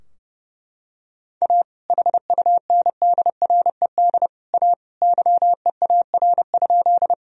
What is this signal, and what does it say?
Morse code, heard as a keyed tone, about 30 wpm, 700 Hz. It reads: A HUNDRED A YEAR?
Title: Sentences from Top 1000 Words - Review Part 2 - 30wpm